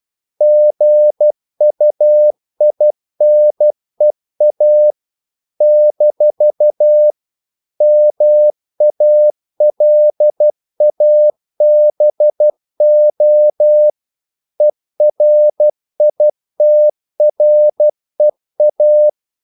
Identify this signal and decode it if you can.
Morse code, heard as a keyed tone, about 12 wpm, 600 Hz. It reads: GUINEA - MALABO ERITREA
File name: CW / Practice / NationalCapitals12